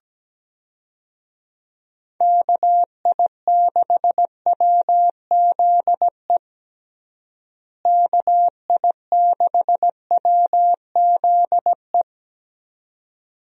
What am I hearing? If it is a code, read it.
Morse code, heard as a keyed tone, about 17 wpm, 700 Hz. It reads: KI6WZE KI6WZE
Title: USA Call Signs - 6 Characters - Repeated 2x - 17wpm